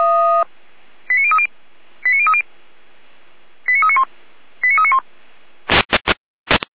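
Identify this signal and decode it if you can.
Five-tone selective call (ZVEI), heard as quick runs of five tones, tones of about 70 ms. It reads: 89030 89030 89391 89391
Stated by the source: Five-tone ZVEI selective call 89140 received on an SDR, fire brigade Orlamünde (feuerwehr-orlamuende).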